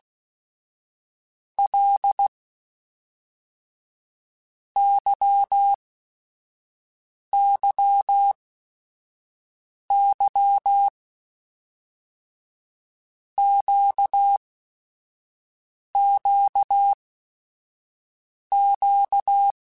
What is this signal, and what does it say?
Morse code, heard as a keyed tone, about 16 wpm, 800 Hz. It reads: L Y Y Y Q Q Q
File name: Basic Sets / Basic Letter Groups Collective7 / LFCYQ@5